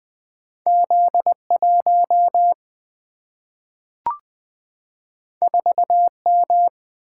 Morse code, heard as a keyed tone, about 20 wpm, 700 Hz. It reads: Z1 4M
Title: Single-Letter Single-Number 20wpm